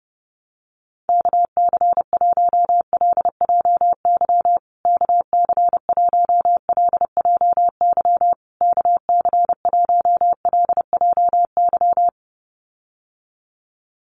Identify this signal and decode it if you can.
Morse code, heard as a keyed tone, about 30 wpm, 700 Hz. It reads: KC1LJY KC1LJY KC1LJY
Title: USA Call Signs Repeated 3x 30wpm